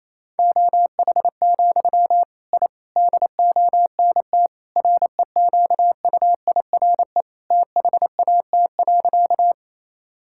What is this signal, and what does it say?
Morse code, heard as a keyed tone, about 28 wpm, 700 Hz. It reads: OH, I DONT REQUIRE THAT.
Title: Sentences from Top 1000 Words - Review Part 6 - 28wpm